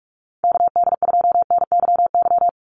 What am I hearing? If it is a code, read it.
Morse code, heard as a keyed tone, about 46 wpm, 690 Hz. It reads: KD2NYY
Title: USA Call Signs 45wpm